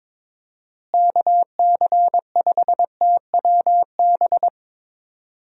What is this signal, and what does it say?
Morse code, heard as a keyed tone, about 22 wpm, 700 Hz. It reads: KC5TWB
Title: USA Call Signs - 6 Characters - Repeated 2x - 22wpm